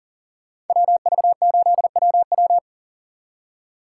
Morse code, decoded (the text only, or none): WV8WW